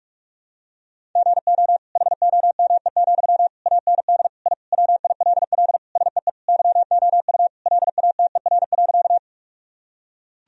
Morse code, heard as a keyed tone, about 46 wpm, 700 Hz. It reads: GO HOME, AND I WILL SEE YOU LATER.
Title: Sentences from Top 1000 Words - Review Part 4 - 45wpm